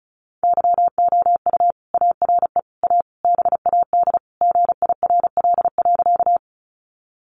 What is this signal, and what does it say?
Morse code, heard as a keyed tone, about 34 wpm, 700 Hz. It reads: YOU ARE A BAD GIRL.
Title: Sentences from Top 700 Words 35wpm